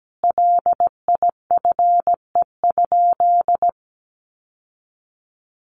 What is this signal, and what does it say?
Morse code, heard as a keyed tone, about 17 wpm, 700 Hz. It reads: LIFE?
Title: Sentences from Top 1300 Words 17wpm